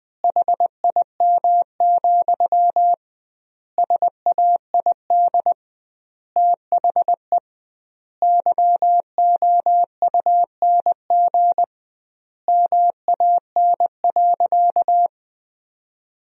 Morse code, heard as a keyed tone, about 21 wpm, 700 Hz. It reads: HIM, SAID THE YOUNG MAN.
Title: 2x Word-Spacing - Sentences from Top 400 Words - 20wpm